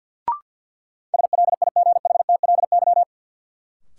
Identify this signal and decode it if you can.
Morse code, heard as a keyed tone, about 53 wpm, 700 Hz. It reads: SLIGHTLY
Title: Sentences from Top 1300 Words - New Word Drill - 50wpm